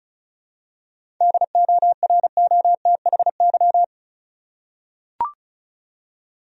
Morse code, heard as a keyed tone, about 34 wpm, 700 Hz. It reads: DOROTHY
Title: USA Common Names - 35wpm